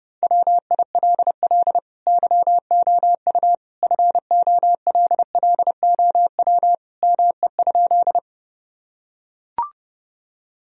Morse code, heard as a keyed tone, about 30 wpm, 700 Hz. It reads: WILL YOU FOLLOW ME?
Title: Sentences from Top 1000 Words - Review Part 1 - 30wpm